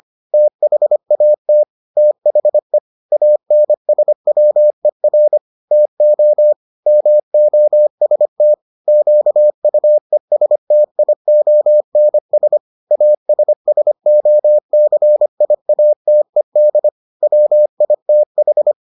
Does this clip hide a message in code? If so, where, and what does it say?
Yes: Morse code from 0.3 to 18.7 s, heard as a keyed tone, about 25 wpm, 600 Hz. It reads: THAT THE ANSWER TO MOST QUESTIONS ASSOCIATED WITH